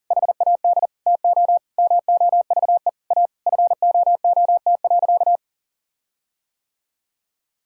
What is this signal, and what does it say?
Morse code, heard as a keyed tone, about 37 wpm, 700 Hz. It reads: HAD TO MOVE A FOOT.
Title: Sentences from Top 1600 Words 40wpm